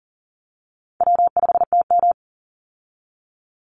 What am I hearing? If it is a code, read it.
Morse code, heard as a keyed tone, about 39 wpm, 700 Hz. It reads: W5TM
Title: Easy Call Signs Repeated 3x - 40wpm